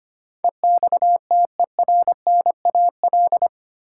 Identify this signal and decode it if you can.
Morse code, heard as a keyed tone, about 25 wpm, 700 Hz. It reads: EXTERNAL
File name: Eight Letter Words - 25wpm